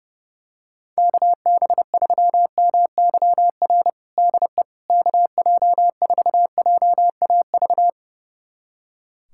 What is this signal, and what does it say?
Morse code, heard as a keyed tone, about 30 wpm, 700 Hz. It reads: KB3MYR DE KJ4JAV
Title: USA Callsign DE Callsign - 6 Characters - 30wpm